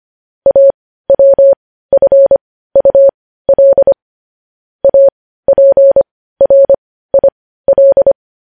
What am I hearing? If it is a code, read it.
Morse code, heard as a keyed tone, about 25 wpm, spaced as for 15 wpm, 550 Hz. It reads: AWFUL APRIL
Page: News Headlines in Morse Code at 15 WPM